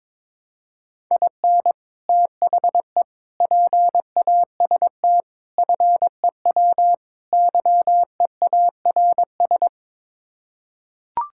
Morse code, heard as a keyed tone, about 22 wpm, 700 Hz. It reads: IN THE PAST FEW YEARS
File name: practice / 22wpm / Build A Phrase - 5-Word Phrases - 22wpm